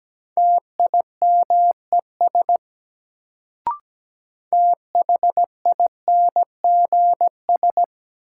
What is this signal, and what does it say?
Morse code, heard as a keyed tone, about 17 wpm, 700 Hz. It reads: TIMES THINGS
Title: Sets of 3 Words - New Word Drill - 17wpm